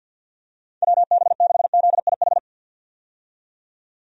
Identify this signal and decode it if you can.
Morse code, heard as a keyed tone, about 52 wpm, 700 Hz. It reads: WB6ZIH
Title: USA Call Signs - 6 Characters - 50wpm